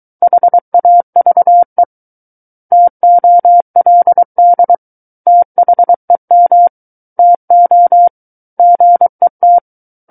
Morse code, heard as a keyed tone, about 23 wpm, 700 Hz. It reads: HAVE TOLD THEM TO GET
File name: Book One / 14-Chapter 14 - 23 WPM